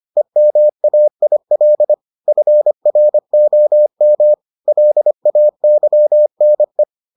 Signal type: Morse code

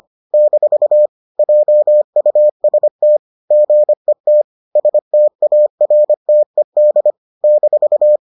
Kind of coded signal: Morse code

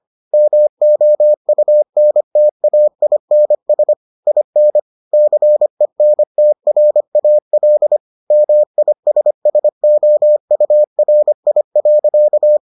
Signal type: Morse code

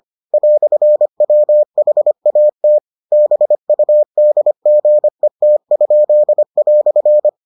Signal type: Morse code

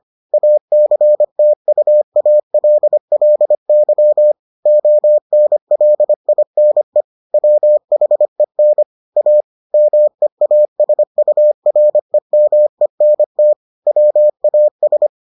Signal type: Morse code

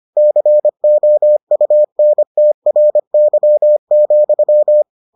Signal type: Morse code

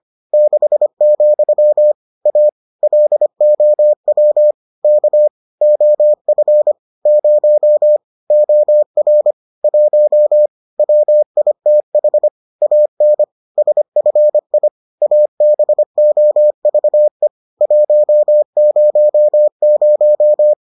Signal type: Morse code

